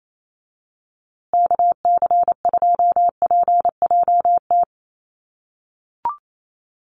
Morse code, heard as a keyed tone, about 28 wpm, 700 Hz. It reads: KC2PJT
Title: USA Call Signs Repeated 2x 28wpm